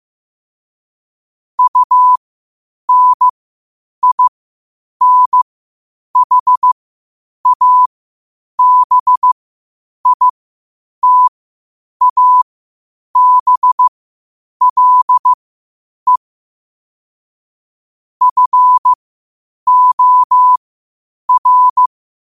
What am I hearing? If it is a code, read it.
Morse code, heard as a keyed tone, about 15 wpm, spaced as for 8 wpm, 1000 Hz. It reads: UNINHABITABLE FOR
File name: qotdfiles / 8wpm